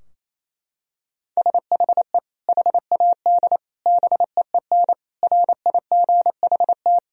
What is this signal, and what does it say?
Morse code, heard as a keyed tone, about 28 wpm, 700 Hz. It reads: SHE HAD BEEN RIGHT
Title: Sentences from Top 1000 Words - Review Part 4 - 28wpm